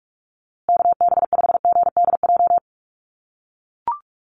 Morse code, heard as a keyed tone, about 44 wpm, 700 Hz. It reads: KB5GDJ
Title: USA Call Signs Repeated 2x 45wpm